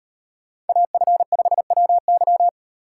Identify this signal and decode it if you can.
Morse code, heard as a keyed tone, about 36 wpm, 700 Hz. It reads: AF5WY